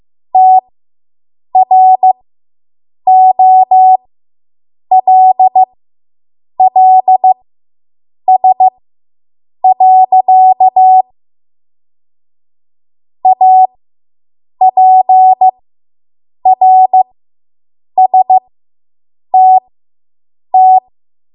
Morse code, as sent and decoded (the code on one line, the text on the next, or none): - .-. --- .-.. .-.. ... .-.-.- / .- .--. .-. ... - -
TROLLS. APRSTT